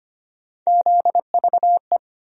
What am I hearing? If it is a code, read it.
Morse code, heard as a keyed tone, about 25 wpm, 700 Hz. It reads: ZVE
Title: USA Call Sign Postfixes - 3 Letters 25wpm